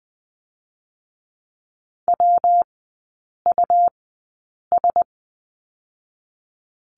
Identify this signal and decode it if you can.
Morse code, heard as a keyed tone, about 19 wpm, spaced as for 8 wpm, 700 Hz. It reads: WUS
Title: 205 Lesson 33e - Letter j - Three Characters 20wpm